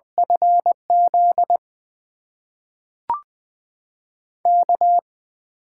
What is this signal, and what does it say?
Morse code, heard as a keyed tone, about 20 wpm, 700 Hz. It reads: FZ K
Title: USA Call Signs - 6 Characters - 8x Character Spacing - 20wpm